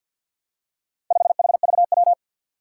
Morse code, heard as a keyed tone, about 47 wpm, 700 Hz. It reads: 5HVW
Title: Four Challenging Letter-Number Combinations 50wpm